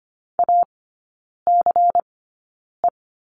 Morse code, heard as a keyed tone, about 25 wpm, spaced as for 9 wpm, 700 Hz. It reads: ACE
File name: course / 25:10wpm / 093 Lesson 16e - Letter c - Three Characters 25wpm